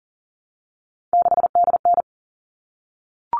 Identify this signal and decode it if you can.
Morse code, heard as a keyed tone, about 38 wpm, 700 Hz. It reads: 6DN
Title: All Cut Numbers - 3-Digit 40wpm